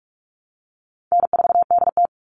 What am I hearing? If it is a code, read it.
Morse code, heard as a keyed tone, about 44 wpm, 700 Hz. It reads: N4DT
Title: Call Signs Repeated 2x - 45wpm